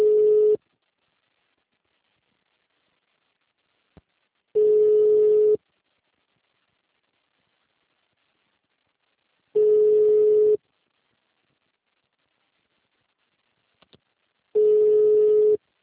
Ringback tone (Europe, 425 Hz).